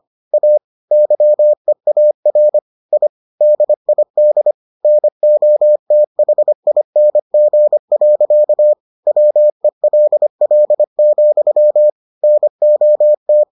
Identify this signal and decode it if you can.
Morse code, heard as a keyed tone, about 25 wpm, 600 Hz. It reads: A YEAR I DID NOTHING. WELL, NOT